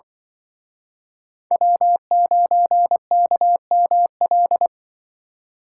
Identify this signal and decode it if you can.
Morse code, heard as a keyed tone, about 25 wpm, 700 Hz. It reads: W9KML